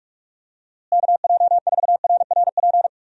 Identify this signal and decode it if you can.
Morse code, heard as a keyed tone, about 45 wpm, 700 Hz. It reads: KJ4RRP
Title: USA Call Signs - 6 Characters - Repeated 2x - 45wpm